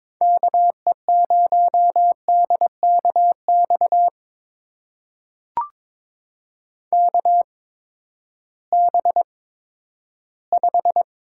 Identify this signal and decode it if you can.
Morse code, heard as a keyed tone, about 22 wpm, spaced as for 12 wpm, 700 Hz. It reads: KE0DKX KB5
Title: USA Call Signs - 6 Characters - 8x Character Spacing - 22wpm